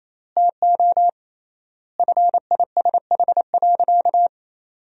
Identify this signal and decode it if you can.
Morse code, heard as a keyed tone, about 28 wpm, 700 Hz. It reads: TO FISH.
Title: 3x Word-Spacing - Sentences from Top 300 Words - 28wpm